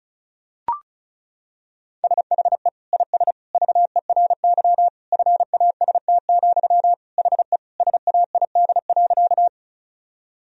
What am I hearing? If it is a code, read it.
Morse code, heard as a keyed tone, about 36 wpm, 700 Hz. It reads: SHE IS VERY FAST, HE SAID.
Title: Sentences from Top 1000 Words - Review Part 2 - 35wpm